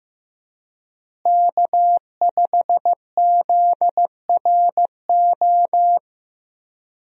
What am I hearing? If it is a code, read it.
Morse code, heard as a keyed tone, about 15 wpm, 700 Hz. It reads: K5ZRO
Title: USA Call Signs - 5 Characters - Repeated 3x - 15wpm